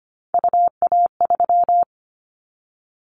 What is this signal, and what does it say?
Morse code, heard as a keyed tone, about 25 wpm, 700 Hz. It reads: UA3
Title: Callsign Prefixes for Top 25 DXCC - 25wpm